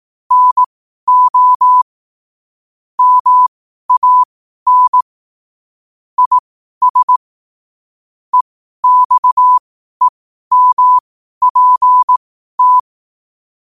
Morse code, heard as a keyed tone, about 18 wpm, spaced as for 12 wpm, 1000 Hz. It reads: NO MAN IS EXEMPT
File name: qotdfiles / 12wpm